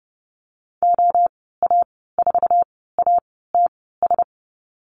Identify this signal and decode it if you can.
Morse code, heard as a keyed tone, about 30 wpm, spaced as for 17 wpm, 700 Hz. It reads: OA4ATS
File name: Lesson 14g - Number 2 - Call Signs 30wpm